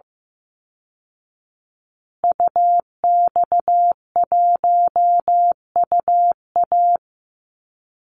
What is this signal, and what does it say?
Morse code, heard as a keyed tone, about 15 wpm, 700 Hz. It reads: UX1UA
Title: Call Signs Repeated 3x - 15wpm